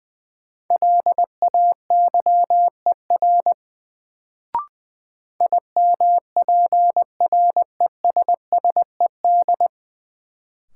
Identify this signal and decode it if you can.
Morse code, heard as a keyed tone, about 20 wpm, 700 Hz. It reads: LAYER IMPRESSED
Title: Sentences from Top 1600 Words - New Word Drill - 20wpm